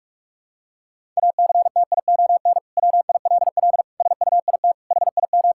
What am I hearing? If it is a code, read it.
Morse code, heard as a keyed tone, about 41 wpm, 700 Hz. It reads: ACTION WILL SUIT HIM
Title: Sentences from Top 1600 Words 45wpm